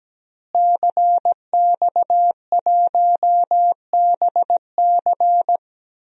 Morse code, heard as a keyed tone, about 17 wpm, 700 Hz. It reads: CX1BC
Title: DXCC-100 Call Signs - 5 Characters - Repeated 2x - 17wpm